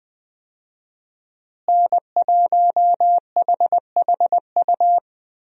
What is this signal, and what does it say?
Morse code, heard as a keyed tone, about 20 wpm, 700 Hz. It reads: N1HHU
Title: USA Call Signs - 5 Characters - Repeated 2x - 20wpm